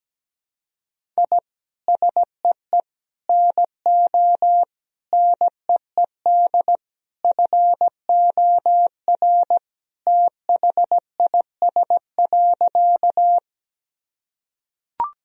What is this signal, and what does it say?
Morse code, heard as a keyed tone, about 17 wpm, 700 Hz. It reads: I SEE NO NEED FOR THIS.
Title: Sentences from Top 1000 Words - Review Part 1 - 17wpm